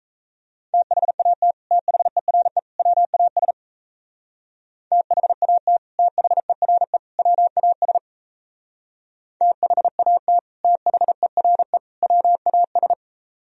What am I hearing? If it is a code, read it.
Morse code, heard as a keyed tone, about 36 wpm, 700 Hz. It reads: THAT THERE WAS THAT THERE WAS THAT THERE WAS